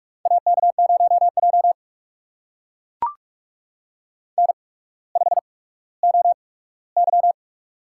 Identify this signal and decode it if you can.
Morse code, heard as a keyed tone, about 44 wpm, spaced as for 19 wpm, 700 Hz. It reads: AK0J N5OY
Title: USA Call Signs - 4 Characters - 8x Character Spacing - 45wpm